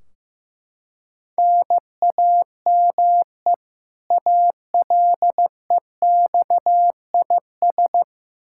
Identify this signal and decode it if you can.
Morse code, heard as a keyed tone, about 15 wpm, 700 Hz. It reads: NAME ALEXIS
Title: Practice QSO - Exchange 1 - Beginning - 15wpm